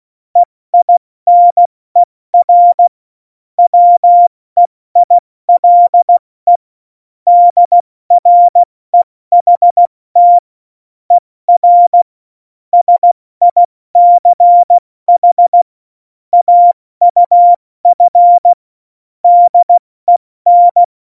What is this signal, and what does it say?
Morse code, heard as a keyed tone, about 16 wpm, spaced as for 14 wpm, 700 Hz. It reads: EINER WEILE DREHT ER SICH AUF DEN
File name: Morsekurs / Witze / Witz 17. Oktober 17, Tempo 70